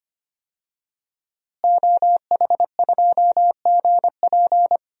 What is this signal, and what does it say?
Morse code, heard as a keyed tone, about 25 wpm, 700 Hz. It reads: OH2GP
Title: DXCC-100 Call Signs - 5 Characters - Repeated 2x - 25wpm